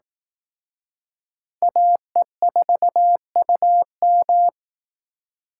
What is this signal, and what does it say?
Morse code, heard as a keyed tone, about 18 wpm, 700 Hz. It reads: AE4UM
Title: Speed Racing - USA 2x2 Call Signs 15wpm